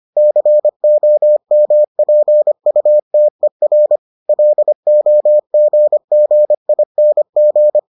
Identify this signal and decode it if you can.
Morse code, heard as a keyed tone, about 25 wpm, 600 Hz. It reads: COMPUTER LOGGING